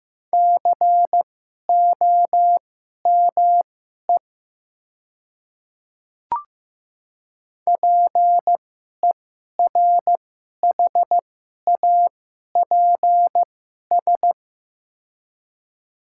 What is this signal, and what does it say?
Morse code, heard as a keyed tone, about 15 wpm, spaced as for 11 wpm, 700 Hz. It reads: COME PERHAPS
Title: 2x Character Spacing - Top 500 Words - 15wpm